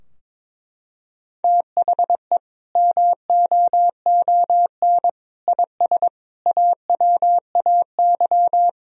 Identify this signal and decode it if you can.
Morse code, heard as a keyed tone, about 22 wpm, 700 Hz. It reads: THE MOON IS AWAY
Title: Sentences from Top 1000 Words - Review Part 3 - 22wpm